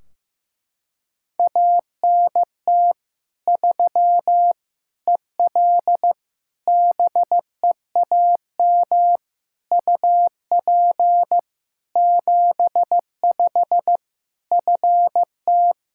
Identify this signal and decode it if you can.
Morse code, heard as a keyed tone, about 16 wpm, 700 Hz. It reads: ANT 3 EL BEAM UP 75 FT
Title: Practice QSO - Exchange 2 - Info - 15wpm